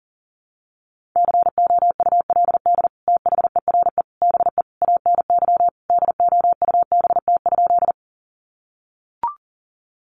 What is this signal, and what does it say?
Morse code, heard as a keyed tone, about 40 wpm, 700 Hz. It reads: COULD THERE BE ANY DOUBT?
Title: Sentences from Top 1500 Words 40wpm